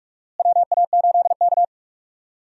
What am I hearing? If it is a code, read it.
Morse code, heard as a keyed tone, about 42 wpm, 700 Hz. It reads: WA8K
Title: USA Call Signs - 4 Characters - Repeated 2x - 45wpm